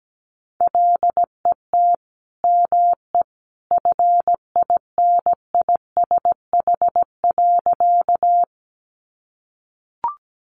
Morse code, heard as a keyed tone, about 17 wpm, 700 Hz. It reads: LET ME FINISH.
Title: Sentences from Top 1200 Words 17wpm